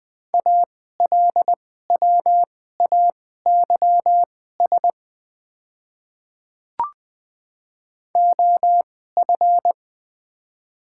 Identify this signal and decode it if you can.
Morse code, heard as a keyed tone, about 20 wpm, spaced as for 14 wpm, 700 Hz. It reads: ALWAYS OF
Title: 2x Character Spacing - Top 300 Words - 20wpm